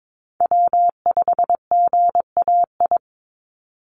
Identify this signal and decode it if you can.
Morse code, heard as a keyed tone, about 22 wpm, 700 Hz. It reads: W5GAI
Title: Call Signs Repeated 2x - 22wpm